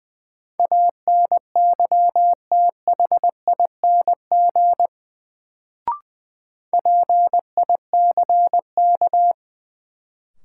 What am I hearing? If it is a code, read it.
Morse code, heard as a keyed tone, about 20 wpm, 700 Hz. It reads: ANYTHING PICK
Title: Sentences from Top 1000 Words - New Word Drill - 20wpm